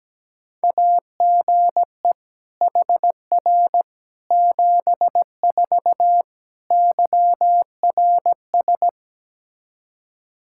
Morse code, heard as a keyed tone, about 17 wpm, 700 Hz. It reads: AGE HR 74 YRS
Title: Practice QSO - Exchange 3 - More Info - 17wpm